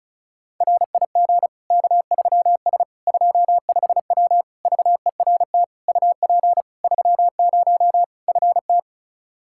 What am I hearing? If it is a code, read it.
Morse code, heard as a keyed tone, about 34 wpm, 700 Hz. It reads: RIG K3S 25W VERT UP 30 FT
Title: Practice QSO - Exchange 2 - Info - 35wpm